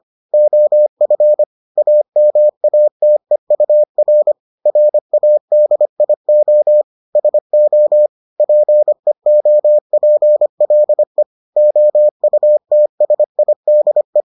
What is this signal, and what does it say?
Morse code, heard as a keyed tone, about 25 wpm, 600 Hz. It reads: OF AMATEUR RADIO SO PEOPLE OUTSIDE